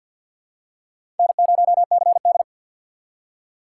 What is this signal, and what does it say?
Morse code, heard as a keyed tone, about 52 wpm, 700 Hz. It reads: N0CD
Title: USA Call Signs - 4 Characters - 3x Character Spacing - 50wpm